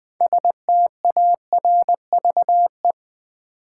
Morse code, heard as a keyed tone, about 20 wpm, 700 Hz. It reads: STARVE
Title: 6 Letter Words - First 3 Letters Given - 20wpm